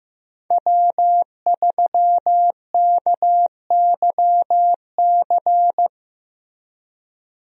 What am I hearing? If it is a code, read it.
Morse code, heard as a keyed tone, about 15 wpm, 700 Hz. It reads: W3KYC
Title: USA Call Signs - 5 Characters - Repeated 2x - 15wpm